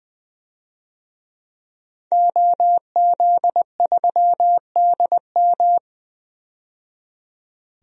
Morse code, heard as a keyed tone, about 21 wpm, 700 Hz. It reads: OZ3DM
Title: DXCC-100 Call Signs - 5 Characters - 20wpm